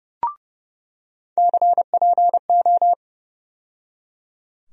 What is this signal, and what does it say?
Morse code, heard as a keyed tone, about 30 wpm, 700 Hz. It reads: CPO
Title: USA Call Sign Postfixes - 3 Letters 30wpm